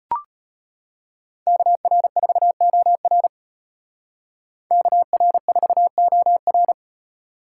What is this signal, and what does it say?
Morse code, heard as a keyed tone, about 30 wpm, 700 Hz. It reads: KR4OR KR4OR
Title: Speed Racing - USA Call Signs 25wpm